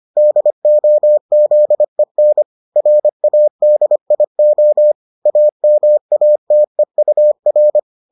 Morse code, heard as a keyed tone, about 25 wpm, 600 Hz. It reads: DOZEN RADIO AMATEUR